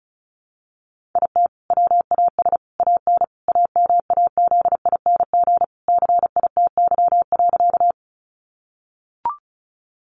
Morse code, heard as a keyed tone, about 34 wpm, 700 Hz. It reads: IT WAS AN AMAZING CITY.
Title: Sentences from Top 1500 Words 35wpm